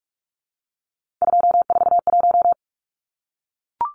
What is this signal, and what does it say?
Morse code, heard as a keyed tone, about 44 wpm, 700 Hz. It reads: 241
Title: Numbers - 3-Digit 45wpm